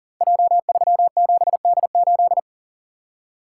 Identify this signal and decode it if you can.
Morse code, heard as a keyed tone, about 40 wpm, 700 Hz. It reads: J37D8
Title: Code Groups - 5 Letter:Numbers - 40wpm